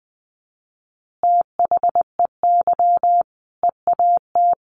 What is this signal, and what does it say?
Morse code, heard as a keyed tone, about 20 wpm, 700 Hz. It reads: THEY EAT
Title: Easy Phrases 20wpm